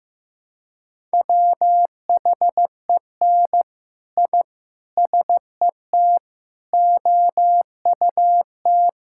Morse code, heard as a keyed tone, about 14 wpm, 700 Hz. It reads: WHEN I SET OUT
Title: Sentences from Top 1000 Words - Review Part 6 - 15wpm